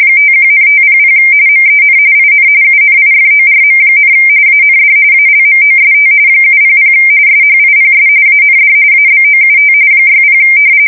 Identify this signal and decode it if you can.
RTTY, about 45.45 baud. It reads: 51,500 MEN IN UNIFORM, AND ALMOST 700 SHIPS AND ABOUT 60
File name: fsk170 / usnavy3